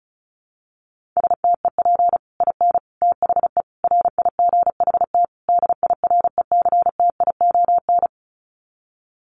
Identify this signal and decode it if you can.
Morse code, heard as a keyed tone, about 34 wpm, 700 Hz. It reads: STEP IN THE RIGHT DIRECTION
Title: Sets of 5 Words Encore 35wpm